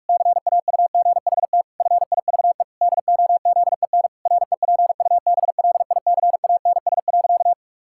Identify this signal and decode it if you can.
Morse code, heard as a keyed tone, about 46 wpm, 700 Hz. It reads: CAUGHT FIVE DOZEN REPUBLICANS.
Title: Pangram Challenge v2 - 45wpm